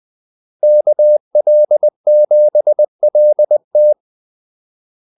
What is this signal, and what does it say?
Morse code, heard as a keyed tone, about 20 wpm, 600 Hz. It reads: KL7LT
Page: Kaldesignaler 20 wpm